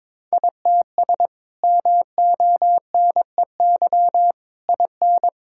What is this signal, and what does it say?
Morse code, heard as a keyed tone, about 22 wpm, 700 Hz. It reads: ITS MONEY IN
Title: Sentences from Top 1000 Words - Review Part 6 - 22wpm